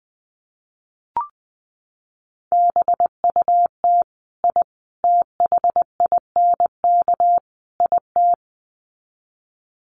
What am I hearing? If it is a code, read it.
Morse code, heard as a keyed tone, about 20 wpm, 700 Hz. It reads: BUT I THINK IT
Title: Sets of 4 Words 20wpm